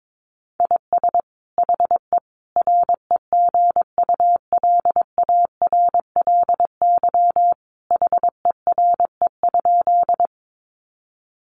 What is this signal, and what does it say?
Morse code, heard as a keyed tone, about 22 wpm, 700 Hz. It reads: IS HE REGULARLY HERE?
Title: Sentences from Top 1400 Words 22wpm